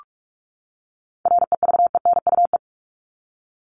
Morse code, heard as a keyed tone, about 44 wpm, 700 Hz. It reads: REVENUE
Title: Top 1501-1750 Words 45wpm